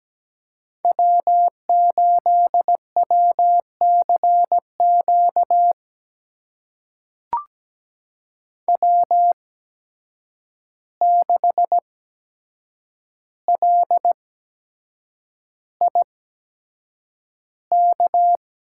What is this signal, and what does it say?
Morse code, heard as a keyed tone, about 17 wpm, spaced as for 7 wpm, 700 Hz. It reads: W8WCQ W6LIK
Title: USA Call Signs - 5 Characters - 8x Character Spacing - 17wpm